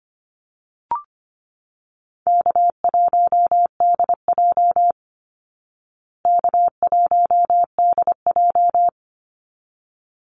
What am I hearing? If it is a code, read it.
Morse code, heard as a keyed tone, about 25 wpm, 700 Hz. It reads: K1DJ K1DJ
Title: Easy Call Signs Repeated 2x - 25wpm